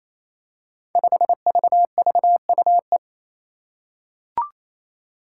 Morse code, heard as a keyed tone, about 28 wpm, 700 Hz. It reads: 5VVUE
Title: Five Challenging Letter-Number Combinations 28wpm